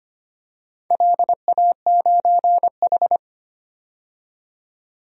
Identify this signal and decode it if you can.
Morse code, heard as a keyed tone, about 25 wpm, 700 Hz. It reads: LA9H
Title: DXCC-100 Call Signs - 4 Characters - 25wpm